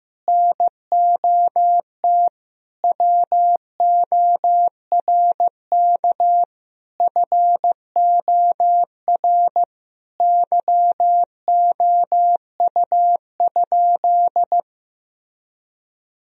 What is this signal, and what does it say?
Morse code, heard as a keyed tone, about 16 wpm, 700 Hz. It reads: NOT WORK FOR YOU?
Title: Sentences from Top 1000 Words - Review Part 4 - 15wpm